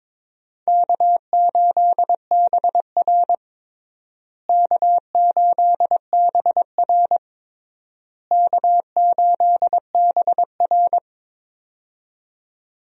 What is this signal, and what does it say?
Morse code, heard as a keyed tone, about 22 wpm, 700 Hz. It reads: K8BR K8BR K8BR
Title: USA Call Signs - 4 Characters - Repeated 3x - 22wpm